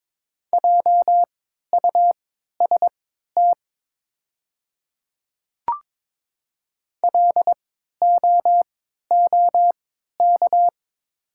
Morse code, heard as a keyed tone, about 22 wpm, spaced as for 13 wpm, 700 Hz. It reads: JUST LOOK
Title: 3x Character Spacing - Top 200 Words - 22wpm